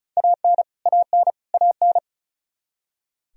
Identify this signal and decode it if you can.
Morse code, heard as a keyed tone, about 36 wpm, 700 Hz. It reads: AN AN AN
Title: Top 100 Words Repeated 35wpm